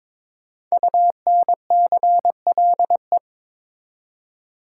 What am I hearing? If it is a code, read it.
Morse code, heard as a keyed tone, about 22 wpm, 700 Hz. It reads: UNCLE
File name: 5 Letter Words - First 3 Letters Given - 22wpm